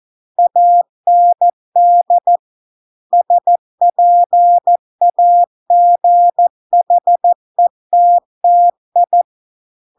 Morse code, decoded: AND SPAGHETTI

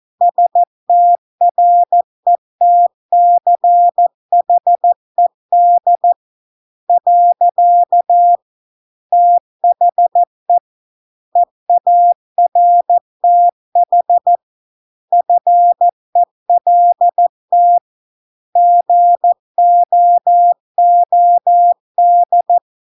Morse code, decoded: STRETCHED . THE EARTH FELT GOOD

